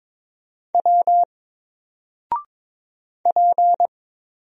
Morse code, heard as a keyed tone, about 23 wpm, 700 Hz. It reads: W P
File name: Single-Letter-Number-22wpm